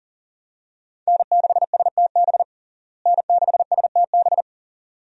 Morse code, heard as a keyed tone, about 40 wpm, 700 Hz. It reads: N6STB N6STB
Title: USA Call Signs - 5 Characters - Repeated 2x - 40wpm